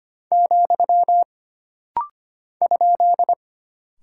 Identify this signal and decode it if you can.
Morse code, heard as a keyed tone, about 25 wpm, 700 Hz. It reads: , ?
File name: Single Punctuation-Prosign - Rapid-Fire - 25wpm